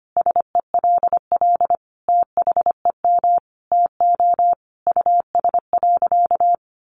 Morse code, heard as a keyed tone, about 25 wpm, 700 Hz. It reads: SELL THEM TO US.
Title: Sentences from Top 1100 Words 25wpm